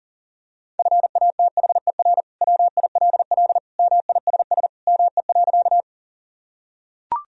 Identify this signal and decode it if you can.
Morse code, heard as a keyed tone, about 40 wpm, 700 Hz. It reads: FATHER WILL MISS ME.
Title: Sentences from Top 1000 Words - Review Part 3 - 40wpm